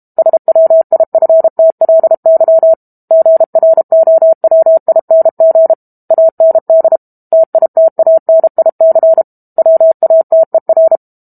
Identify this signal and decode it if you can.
Morse code, heard as a keyed tone, about 32 wpm, 650 Hz. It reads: SWIFTLY GROWING AND TITANIC WATER
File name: Book Two / 06-Chapter 6 - 32 WPM